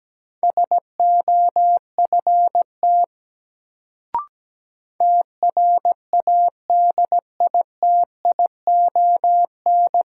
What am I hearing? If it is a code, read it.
Morse code, heard as a keyed tone, about 17 wpm, 700 Hz. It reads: SOFT TRADITION
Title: Sentences from Top 1500 Words - New Word Drill Alt - 17wpm